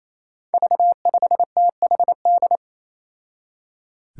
Morse code, heard as a keyed tone, about 28 wpm, 700 Hz. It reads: V5THD
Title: Five Challenging Letter-Number Combinations 28wpm